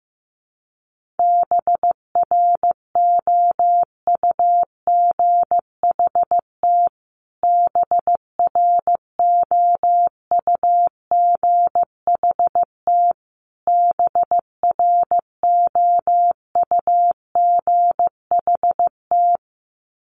Morse code, heard as a keyed tone, about 15 wpm, 700 Hz. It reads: BROUGHT BROUGHT BROUGHT
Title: Top 500 Words Repeated 15wpm